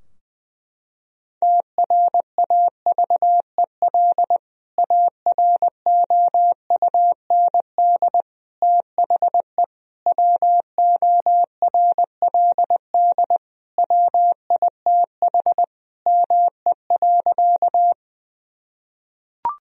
Morse code, heard as a keyed tone, about 21 wpm, 700 Hz. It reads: TRAVEL AROUND THE WORLD WITH ME.
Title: Sentences from Top 1000 Words - Review Part 3 - 20wpm